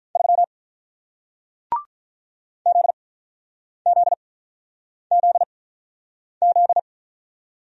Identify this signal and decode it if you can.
Morse code, heard as a keyed tone, about 46 wpm, spaced as for 9 wpm, 700 Hz. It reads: 3 ZZZZ